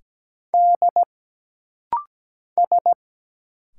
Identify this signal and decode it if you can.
Morse code, heard as a keyed tone, about 17 wpm, 700 Hz. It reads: D S
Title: Single Letter-Number - Rapid-Fire 17wpm